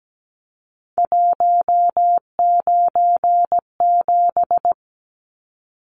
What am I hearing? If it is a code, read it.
Morse code, heard as a keyed tone, about 17 wpm, 700 Hz. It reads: 197
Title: Numbers - 3-Digit 17wpm